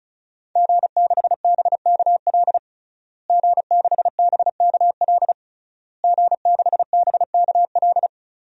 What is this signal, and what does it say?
Morse code, heard as a keyed tone, about 34 wpm, 700 Hz. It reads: G6BKL G6BKL G6BKL
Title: DXCC-100 Call Signs - 5 Characters - Repeated 3x - 35wpm